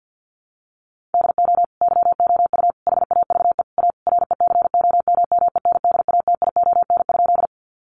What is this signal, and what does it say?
Morse code, heard as a keyed tone, about 50 wpm, 700 Hz. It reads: DO YOU HAVE A RECOMMENDATION?
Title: Sentences from Top 1400 Words 50wpm